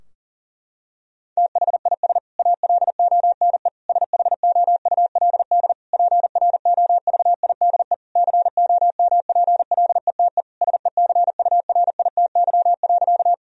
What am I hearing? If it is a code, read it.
Morse code, heard as a keyed tone, about 40 wpm, 700 Hz. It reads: THIS ALONE SHOULD PROVIDE COMPLETE SECURITY.
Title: Sentences from Top 1000 Words - Review Part 4 - 40wpm